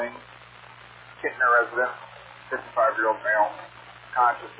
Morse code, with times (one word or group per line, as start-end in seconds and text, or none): none